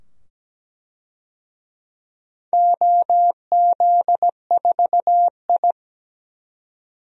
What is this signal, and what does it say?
Morse code, heard as a keyed tone, about 17 wpm, 700 Hz. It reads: OZ4I
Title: DXCC-100 Call Signs - 4 Characters - 17wpm